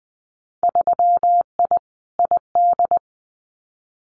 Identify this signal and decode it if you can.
Morse code, heard as a keyed tone, about 20 wpm, 700 Hz. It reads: 3I ID
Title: Winter Field Day Exchanges 20wpm